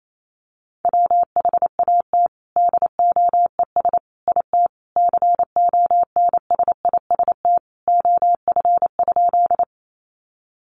Morse code, heard as a keyed tone, about 28 wpm, 700 Hz. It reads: WHAT DOES IT CONSIST OF?
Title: Sentences from Top 1400 Words 28wpm